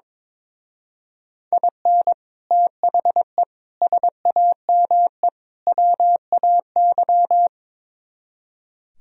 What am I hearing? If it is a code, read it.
Morse code, heard as a keyed tone, about 24 wpm, 700 Hz. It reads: IN THE SAME WAY
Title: Speed-Racing - Sets of 4 Words 22wpm